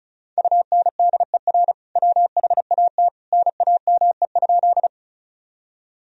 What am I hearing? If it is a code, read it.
Morse code, heard as a keyed tone, about 36 wpm, 700 Hz. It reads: UNDER WHAT NAME?
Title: Sentences from Top 1000 Words - Review Part 1 - 35wpm